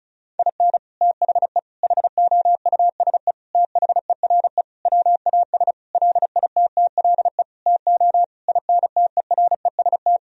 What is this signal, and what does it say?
Morse code, heard as a keyed tone, about 34 wpm, 700 Hz. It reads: IN THE HOUSE THERE WAS LITTLE TO INTEREST
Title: Sentences from Top 1000 Words - Review Part 2 - 35wpm